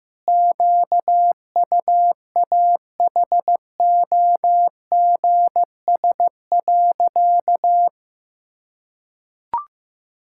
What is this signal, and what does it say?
Morse code, heard as a keyed tone, about 15 wpm, 700 Hz. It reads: QUAHOGS.
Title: Pangram Challenge v2 - 15wpm